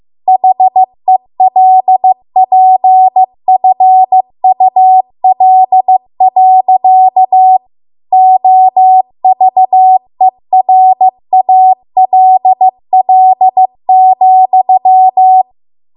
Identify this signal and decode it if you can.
Morse code, heard as a keyed tone, about 15 wpm, 750 Hz. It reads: HELPFUL. OVERALL,
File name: Archive / 15 WPM / 15WPM